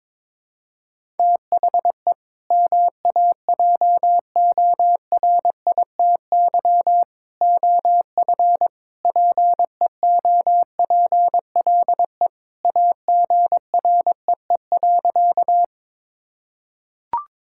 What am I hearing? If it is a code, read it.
Morse code, heard as a keyed tone, about 22 wpm, 700 Hz. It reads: THE MAJORITY OF PEOPLE AGREE.
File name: practice / 22wpm / Sentences with 3-Word Phrases - 22wpm